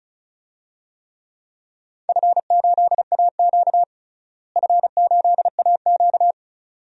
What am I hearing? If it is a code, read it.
Morse code, heard as a keyed tone, about 36 wpm, 700 Hz. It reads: F8AQ F8AQ
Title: DXCC-100 Call Signs - 4 Characters - Repeated 2x - 35wpm